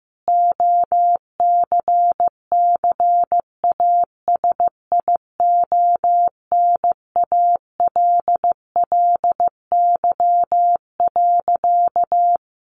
Morse code, heard as a keyed tone, about 15 wpm, 700 Hz. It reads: OCCASIONALLY.
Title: Sentences from Top 1300 Words 15wpm